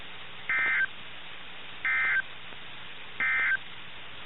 Emergency Alert System (SAME) tones, heard as harsh buzzing bursts. It is heard from 0.4 to 3.7 s.
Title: NOAA Weather Radio - KIH61